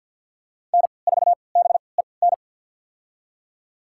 Morse code, heard as a keyed tone, about 46 wpm, spaced as for 27 wpm, 700 Hz. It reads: N4BEN
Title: USA Call Signs - 5 Characters - 3x Character Spacing - 50wpm